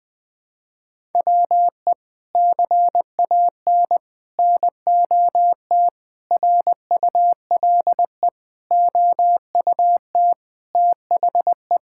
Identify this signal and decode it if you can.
Morse code, heard as a keyed tone, about 20 wpm, 700 Hz. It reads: WE CAN NOT RULE OUT THE
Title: Sets of 8 Words - 20wpm